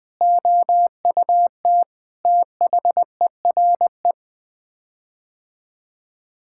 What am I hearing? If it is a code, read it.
Morse code, heard as a keyed tone, about 21 wpm, 700 Hz. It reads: OUT THERE
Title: Sets of 8 Words - 20wpm